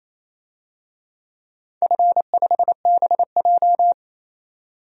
Morse code, heard as a keyed tone, about 28 wpm, 700 Hz. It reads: F5BJ
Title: DXCC-100 Call Signs - 4 Characters - 28wpm